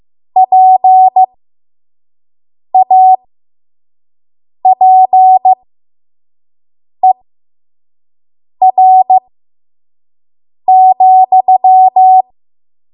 Morse code, heard as a keyed tone, about 15 wpm, spaced as for 5 wpm, 750 Hz. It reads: PAPER,
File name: Morse / 05WPM